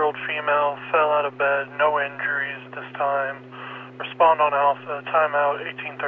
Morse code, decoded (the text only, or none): none